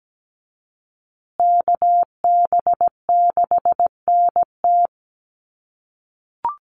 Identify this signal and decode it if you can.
Morse code, heard as a keyed tone, about 17 wpm, 700 Hz. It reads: KB6NT
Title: USA Call Signs Repeated 3x 17wpm